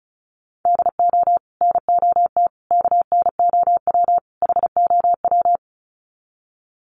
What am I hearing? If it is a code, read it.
Morse code, heard as a keyed tone, about 34 wpm, 700 Hz. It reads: DO NOT KNOW HOW
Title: Sets of 4 Words 35wpm